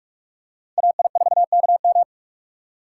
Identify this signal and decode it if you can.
Morse code, heard as a keyed tone, about 44 wpm, 700 Hz. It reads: AI4KM
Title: USA Call Signs - 5 Characters - 6x Character Spacing - 45wpm